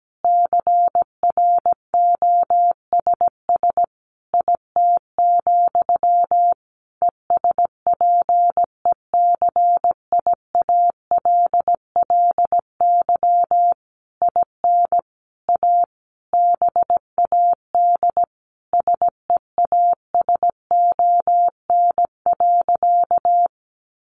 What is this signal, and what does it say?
Morse code, heard as a keyed tone, about 17 wpm, 700 Hz. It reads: CROSS IT, ESPECIALLY IN A BAD SEASON.
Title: Sentences from Top 1400 Words 17wpm